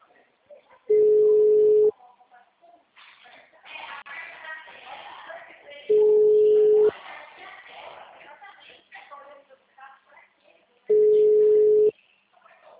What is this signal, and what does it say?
Ringback tone (Europe, 425 Hz), with speech.
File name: out- -2011- - - .87